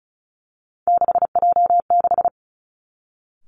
Morse code, heard as a keyed tone, about 36 wpm, 700 Hz. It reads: 6J6
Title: Three Challenging Letter-Number Combinations 35wpm